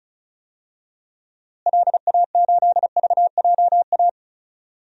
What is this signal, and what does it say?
Morse code, heard as a keyed tone, about 36 wpm, 700 Hz. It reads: LA8VJA
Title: DXCC-100 Call Signs - 6 Characters - Repeated 3x - 35wpm